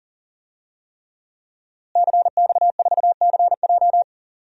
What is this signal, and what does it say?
Morse code, heard as a keyed tone, about 40 wpm, 700 Hz. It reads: CX4CJ
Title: DXCC-100 Call Signs - 5 Characters - Repeated 3x - 40wpm